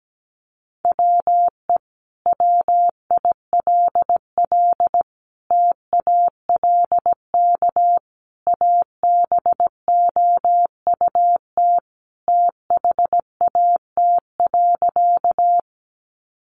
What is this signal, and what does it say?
Morse code, heard as a keyed tone, about 17 wpm, 700 Hz. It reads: WE WILL TALK ABOUT THAT.
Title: Sentences from Top 400 Words 17wpm